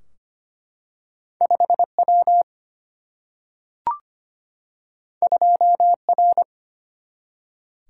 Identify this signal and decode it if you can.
Morse code, heard as a keyed tone, about 25 wpm, 700 Hz. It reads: 5W 2R
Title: Single-Letter Single-Number 25wpm